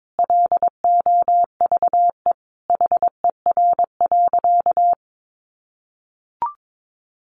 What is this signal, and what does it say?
Morse code, heard as a keyed tone, about 22 wpm, 700 Hz. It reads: LOVE HER.
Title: Sentences from Top 400 Words 22wpm